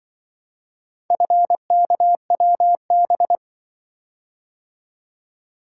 Morse code, heard as a keyed tone, about 24 wpm, 700 Hz. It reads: FKWB
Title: Speed-Racing - Four Letter-Number Combinations - 20wpm